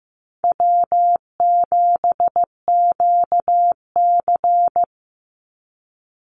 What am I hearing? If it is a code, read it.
Morse code, heard as a keyed tone, about 15 wpm, 700 Hz. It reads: W7QC
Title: Easy Call Signs - 15wpm